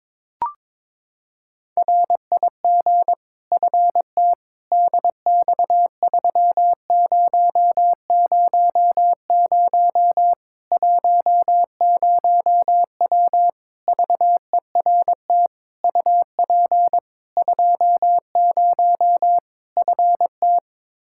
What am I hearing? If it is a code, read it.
Morse code, heard as a keyed tone, about 22 wpm, 700 Hz. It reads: RIG FT DX3000 10W VERT UP 20 FT
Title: Practice QSO - Exchange 2 - Info - 22wpm